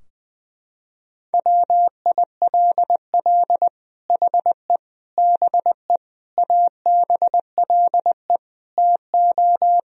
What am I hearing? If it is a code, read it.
Morse code, heard as a keyed tone, about 20 wpm, 700 Hz. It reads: WILL HE BE ABLE TO
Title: Sentences from Top 1000 Words - Review Part 3 - 20wpm